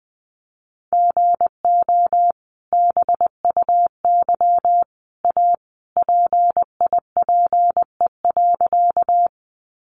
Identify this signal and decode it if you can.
Morse code, heard as a keyed tone, about 20 wpm, 700 Hz. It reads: GO BUY A PIPE.
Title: Sentences from Top 1500 Words 20wpm